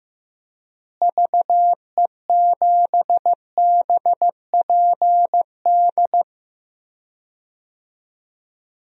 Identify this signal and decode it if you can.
Morse code, heard as a keyed tone, about 14 wpm, 700 Hz. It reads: VE7BPD
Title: DXCC-100 Call Signs - 6 Characters - Repeated 3x - 15wpm